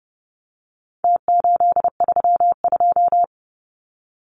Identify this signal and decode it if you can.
Morse code, heard as a keyed tone, about 30 wpm, 700 Hz. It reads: T832
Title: Numbers - Contest Style - 4-Digit 30wpm